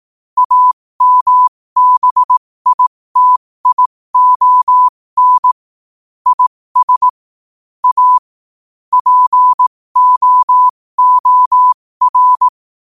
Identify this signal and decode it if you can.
Morse code, heard as a keyed tone, about 18 wpm, spaced as for 15 wpm, 1000 Hz. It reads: AMBITION IS A POOR
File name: qotdfiles / 15wpm